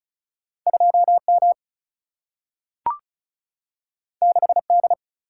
Morse code, heard as a keyed tone, about 36 wpm, 700 Hz. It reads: 2 M 6 D
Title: Single-Letter Single-Number 35wpm